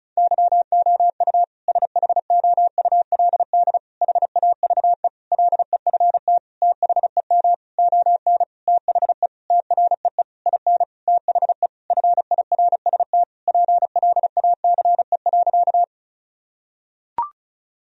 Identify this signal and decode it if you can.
Morse code, heard as a keyed tone, about 36 wpm, 700 Hz. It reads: YOU SHOULD HAVE LEFT THEM ON THE TREE IN THE FIRST PLACE.
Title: Sentences from Top 1000 Words - Review Part 1 - 35wpm